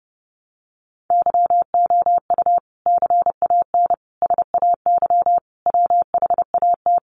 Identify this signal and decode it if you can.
Morse code, heard as a keyed tone, about 30 wpm, 700 Hz. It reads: YOU CAN SAY WHAT
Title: Sentences from Top 200 Words 30wpm